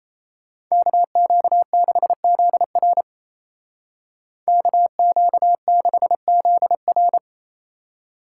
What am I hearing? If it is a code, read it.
Morse code, heard as a keyed tone, about 34 wpm, 700 Hz. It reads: KQ6ZR KQ6ZR
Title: Speed Racing - USA Call Signs 28wpm